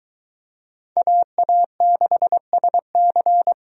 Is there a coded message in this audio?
Yes: Morse code from 1.0 to 3.5 s, heard as a keyed tone, about 22 wpm, 700 Hz. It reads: AA6SC